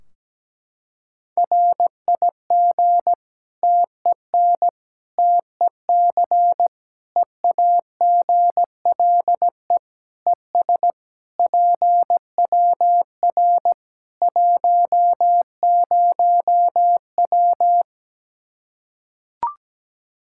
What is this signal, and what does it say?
Morse code, heard as a keyed tone, about 17 wpm, 700 Hz. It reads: RIG TEN TEC EAGLE ES PWR 10W
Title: K1USN SST QSO Practice 17wpm